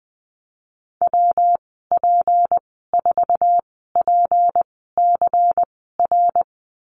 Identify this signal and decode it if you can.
Morse code, heard as a keyed tone, about 20 wpm, spaced as for 15 wpm, 700 Hz. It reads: WP4PCR
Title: 179 Lesson 29g - Slash - Call Signs 20wpm